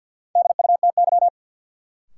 Morse code, heard as a keyed tone, about 52 wpm, 700 Hz. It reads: DUTY
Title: Sentences from Top 1500 Words - New Word Drill Alt - 50wpm